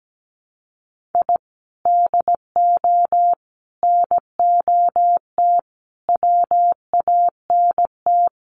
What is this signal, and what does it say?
Morse code, heard as a keyed tone, about 17 wpm, 700 Hz. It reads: I DO NOT WANT
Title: Sets of 4 Words 17wpm